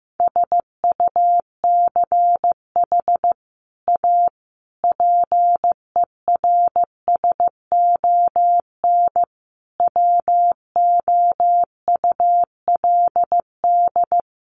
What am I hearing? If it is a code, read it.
Morse code, heard as a keyed tone, about 14 wpm, 700 Hz. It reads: SUCH A PERSON WOULD
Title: Sentences from Top 1400 Words 15wpm